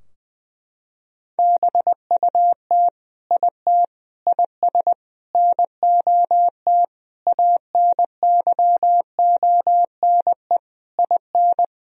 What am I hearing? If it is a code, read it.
Morse code, heard as a keyed tone, about 21 wpm, 700 Hz. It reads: BUT IT IS NOT ANYONE IN
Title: Sentences from Top 1000 Words - Review Part 4 - 20wpm